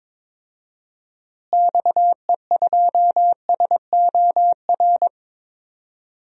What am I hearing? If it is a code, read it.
Morse code, heard as a keyed tone, about 22 wpm, 700 Hz. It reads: XE2SOR
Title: DXCC-100 Call Signs - 6 Characters - Repeated 2x - 22wpm